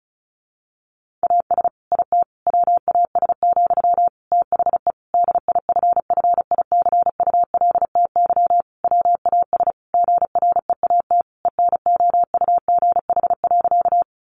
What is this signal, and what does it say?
Morse code, heard as a keyed tone, about 34 wpm, 700 Hz. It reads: AS IT WAS, THE DIFFICULTY WAS GREAT ENOUGH.
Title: Sentences from Top 1500 Words 35wpm